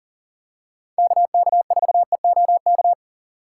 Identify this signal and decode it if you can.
Morse code, heard as a keyed tone, about 40 wpm, 700 Hz. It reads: KK4EOK
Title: USA Call Signs - 6 Characters - 3x Character Spacing - 40wpm